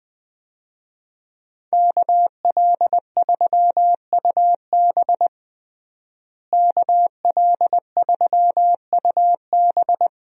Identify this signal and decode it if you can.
Morse code, heard as a keyed tone, about 20 wpm, 700 Hz. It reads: KL3UB KL3UB
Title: USA Call Signs - 5 Characters - Repeated 2x - 20wpm